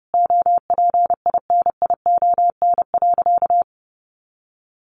Morse code, heard as a keyed tone, about 30 wpm, 700 Hz. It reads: OPINION.
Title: Sentences from Top 1400 Words 30wpm